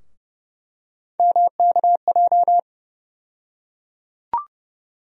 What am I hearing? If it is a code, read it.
Morse code, heard as a keyed tone, about 30 wpm, 700 Hz. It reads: MKJ
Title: USA Call Sign Postfixes - 3 Letters 30wpm